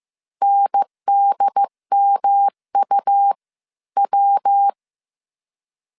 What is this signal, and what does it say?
Morse code, heard as a keyed tone, about 15 wpm, spaced as for 12 wpm, 800 Hz. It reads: NDMUW
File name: CW / DET / 1 3 1500C S C 14wpm 2458